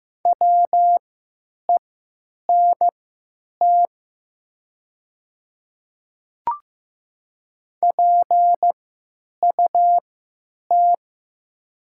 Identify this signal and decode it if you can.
Morse code, heard as a keyed tone, about 15 wpm, spaced as for 9 wpm, 700 Hz. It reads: WENT PUT
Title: 3x Character Spacing - Top 200 Words - 15wpm